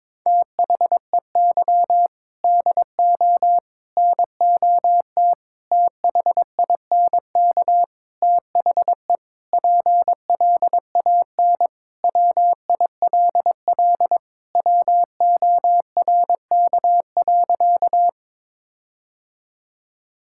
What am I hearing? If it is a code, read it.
Morse code, heard as a keyed tone, about 22 wpm, 700 Hz. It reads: THEY DO NOT THINK THE PLAN WILL WORK.
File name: practice / 22wpm / Sentences with 4-Word Phrases - 22wpm